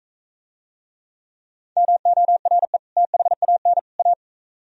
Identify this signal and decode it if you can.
Morse code, heard as a keyed tone, about 42 wpm, 700 Hz. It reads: MORE THAN A